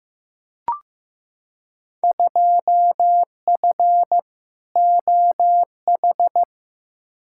Morse code, heard as a keyed tone, about 15 wpm, 700 Hz. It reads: 2F OH
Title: Field Day Exchanges - 15wpm